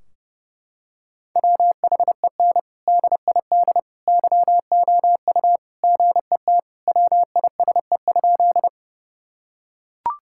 Morse code, heard as a keyed tone, about 30 wpm, 700 Hz. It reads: WHEN DID YOU GET WISE?
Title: Sentences from Top 1600 Words 30wpm